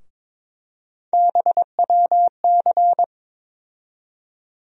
Morse code, heard as a keyed tone, about 22 wpm, 700 Hz. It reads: BWC
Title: USA Call Sign Postfixes - 3 Letters 22wpm